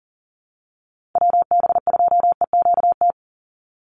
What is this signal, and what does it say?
Morse code, heard as a keyed tone, about 38 wpm, 700 Hz. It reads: WB2EQT
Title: USA Call Signs 40wpm